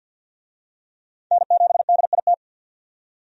Morse code, heard as a keyed tone, about 52 wpm, 700 Hz. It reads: N7DIT
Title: USA Call Signs - 5 Characters - 50wpm